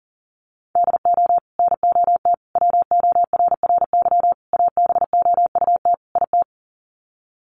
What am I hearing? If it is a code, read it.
Morse code, heard as a keyed tone, about 40 wpm, 700 Hz. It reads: DO NOT WORRY ABOUT IT
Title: Sets of 5 Words 40wpm